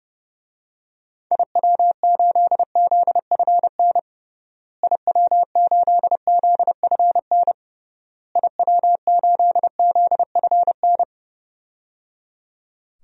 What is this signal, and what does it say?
Morse code, heard as a keyed tone, about 30 wpm, 700 Hz. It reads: IW8ZFN IW8ZFN IW8ZFN
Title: DXCC-100 Call Signs - 6 Characters - Repeated 3x - 30wpm